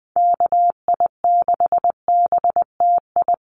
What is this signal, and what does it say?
Morse code, heard as a keyed tone, about 20 wpm, 700 Hz. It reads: KI6BTI
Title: USA Call Signs Repeated 2x 20wpm